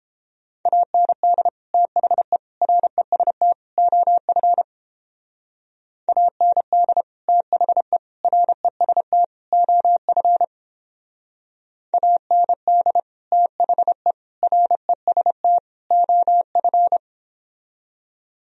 Morse code, heard as a keyed tone, about 29 wpm, 700 Hz. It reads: AND THE REST OF AND THE REST OF AND THE REST OF